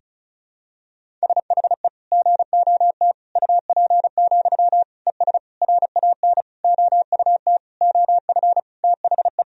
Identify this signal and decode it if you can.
Morse code, heard as a keyed tone, about 34 wpm, 700 Hz. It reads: SHE GOT UP, ES RAN OUT OF THE
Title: Sentences from Top 500 Words and Abbreviations 35wpm